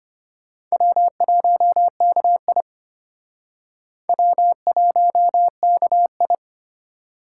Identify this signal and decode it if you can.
Morse code, heard as a keyed tone, about 29 wpm, 700 Hz. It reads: W1KI W1KI
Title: Speed Racing - USA 1x2 Call Signs 25wpm